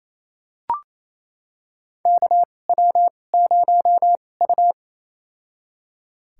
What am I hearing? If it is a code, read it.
Morse code, heard as a keyed tone, about 28 wpm, spaced as for 20 wpm, 690 Hz. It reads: KW0U
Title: USA Call Signs - 4 Characters - 2x Character Spacing - 28wpm